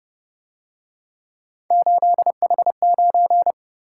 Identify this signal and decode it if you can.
Morse code, heard as a keyed tone, about 28 wpm, 700 Hz. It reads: 8H9